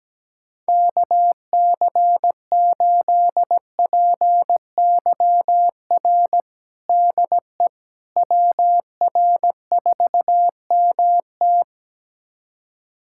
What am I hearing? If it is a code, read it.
Morse code, heard as a keyed tone, about 17 wpm, 700 Hz. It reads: KC8PYR DE WR4MT
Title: USA Callsign DE Callsign 17wpm